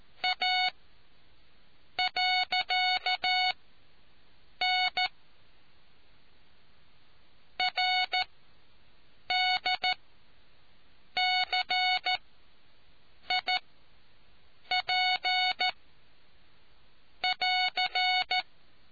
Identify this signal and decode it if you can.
Morse code, heard as a keyed tone, about 14 wpm, 740 Hz. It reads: A . N R D C I P <AR>